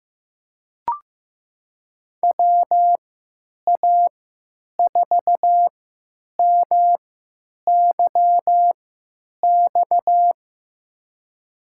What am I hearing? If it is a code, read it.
Morse code, heard as a keyed tone, about 15 wpm, spaced as for 9 wpm, 700 Hz. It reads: WA4MYX
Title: USA Call Signs - 6 Characters - 3x Character Spacing - 15wpm